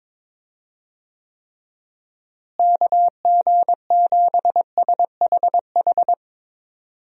Morse code, heard as a keyed tone, about 22 wpm, 700 Hz. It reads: KG7SHH
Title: USA Call Signs - 6 Characters - Repeated 3x - 22wpm